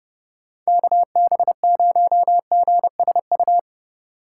Morse code, heard as a keyed tone, about 29 wpm, 700 Hz. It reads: KB0GSU